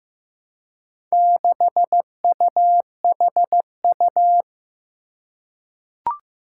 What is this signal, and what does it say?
Morse code, heard as a keyed tone, about 15 wpm, 700 Hz. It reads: 6UHU
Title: Four Challenging Letter-Number Combinations 15wpm